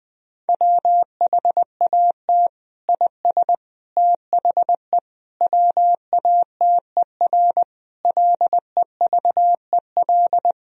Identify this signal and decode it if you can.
Morse code, heard as a keyed tone, about 21 wpm, 700 Hz. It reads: WHAT IS THE WATER LEVEL
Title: Sentences from Top 1000 Words - Review Part 4 - 20wpm